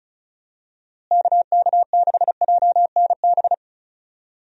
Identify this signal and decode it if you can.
Morse code, heard as a keyed tone, about 36 wpm, 700 Hz. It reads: KK6JNB
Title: USA Call Signs - 6 Characters - Repeated 3x - 35wpm